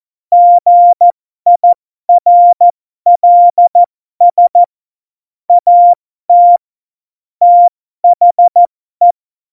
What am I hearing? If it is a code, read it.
Morse code, heard as a keyed tone, about 14 wpm, spaced as for 12 wpm, 700 Hz. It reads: GIRLS AT THE
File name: Morsekurs / Witze